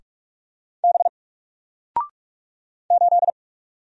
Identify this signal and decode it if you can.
Morse code, heard as a keyed tone, about 44 wpm, 700 Hz. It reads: B 8
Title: Single Character (Letter:Number:Punctuation:Prosign) Alt 45wpm